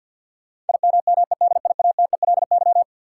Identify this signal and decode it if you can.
Morse code, heard as a keyed tone, about 52 wpm, 700 Hz. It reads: IMMEDIATELY
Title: Sentences from Top 1100 Words - New Word Drill Alt - 50wpm